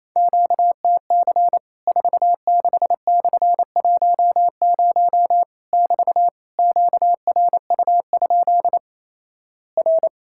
Morse code, heard as a keyed tone, about 28 wpm, 700 Hz. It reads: QTC 46/10 <BT> QRU? R
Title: Worked All Europe DX Contest - QTC Exchange - 28wpm